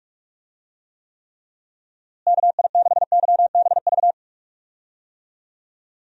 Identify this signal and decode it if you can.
Morse code, heard as a keyed tone, about 41 wpm, 700 Hz. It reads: KI6YBV
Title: USA Call Signs - 6 Characters - Repeated 3x - 45wpm